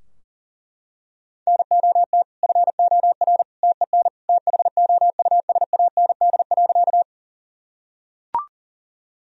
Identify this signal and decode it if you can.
Morse code, heard as a keyed tone, about 40 wpm, 700 Hz. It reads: NOT FOR TEN THOUSAND.
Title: Sentences from Top 1000 Words - Review Part 3 - 40wpm